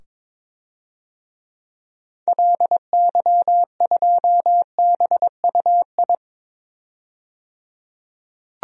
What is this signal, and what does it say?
Morse code, heard as a keyed tone, about 22 wpm, 700 Hz. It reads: LY2BUI
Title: DXCC-100 Call Signs - 6 Characters - 22wpm